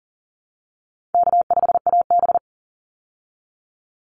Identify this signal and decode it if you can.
Morse code, heard as a keyed tone, about 40 wpm, 700 Hz. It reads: K5AB
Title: Easy Call Signs Repeated 2x - 40wpm